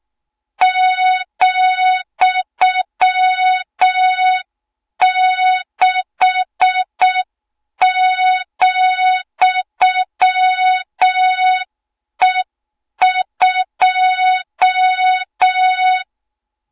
Morse code, heard as a keyed tone, about 6 wpm, 3700 Hz. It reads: ,6,E2
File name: Morze-HA2MN / hang / 20 GYAKORLAT